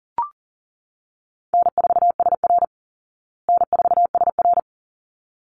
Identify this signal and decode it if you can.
Morse code, heard as a keyed tone, about 40 wpm, 700 Hz. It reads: N4SR N4SR
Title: Call Signs Repeated 3x - 40wpm